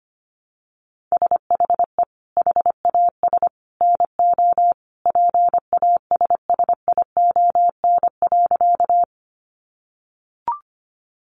Morse code, heard as a keyed tone, about 25 wpm, 700 Hz. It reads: SHE HAS NO PASSION.
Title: Sentences from Top 1300 Words 25wpm